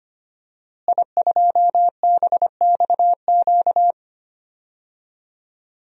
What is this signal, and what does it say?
Morse code, heard as a keyed tone, about 25 wpm, 700 Hz. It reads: I2BXQ
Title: DXCC-100 Call Signs - 5 Characters - Repeated 3x - 25wpm